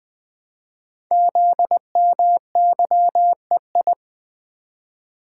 Morse code, heard as a keyed tone, about 20 wpm, 700 Hz. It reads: ZMYEI
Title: Code Groups - 5 Letters - 20wpm